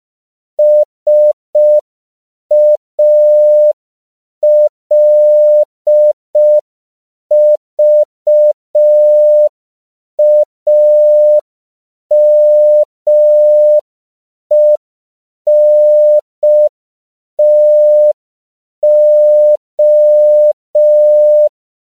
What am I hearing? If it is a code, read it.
Morse code, heard as a keyed tone, about 5 wpm, 600 Hz. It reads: SALVAMENTO
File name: CW / 5PPM